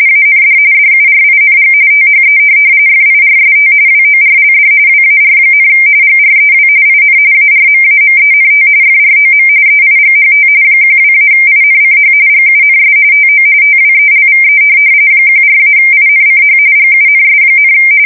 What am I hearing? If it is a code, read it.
RTTY, about 45.45 baud. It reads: AND DURING THE FIRST WORLD WAR, GREW IN ARMAMENT FROM EIGHT 12-INCH GUNS TO TWELVE 14-INCH GUNS,
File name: fsk170 / USS-New-Jersey